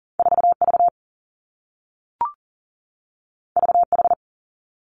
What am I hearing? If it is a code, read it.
Morse code, heard as a keyed tone, about 41 wpm, 700 Hz. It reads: 4 V V H
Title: Two Challenging Letter-Number Combinations 40wpm